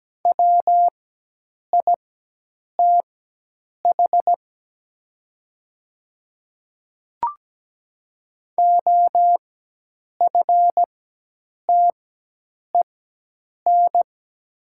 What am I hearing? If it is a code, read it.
Morse code, heard as a keyed tone, about 17 wpm, spaced as for 8 wpm, 700 Hz. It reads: WITH OFTEN
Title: 4x Character Spacing - Top 500 Words - 17wpm